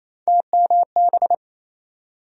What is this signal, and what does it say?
Morse code, heard as a keyed tone, about 27 wpm, 700 Hz. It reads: TMB
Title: USA Call Sign Postfixes - 3 Letters 28wpm